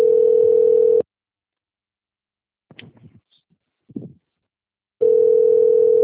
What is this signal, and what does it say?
Ringback tone (North America), 440 + 480 Hz.
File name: out- - - - - .42033